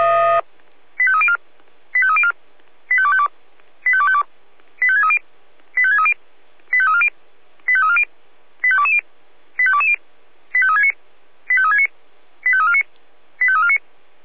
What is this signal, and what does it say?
Five-tone selective call (ZVEI), heard as quick runs of five tones, tones of about 70 ms. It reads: 85304 85304 85252 85252 85630 85630 85340 85340 85220 85220 85369 85369 85349 85349